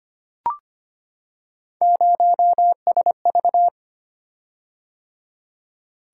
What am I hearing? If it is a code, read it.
Morse code, heard as a keyed tone, about 25 wpm, 700 Hz. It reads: 0SV
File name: Three Letter-Number Combinations - 25wpm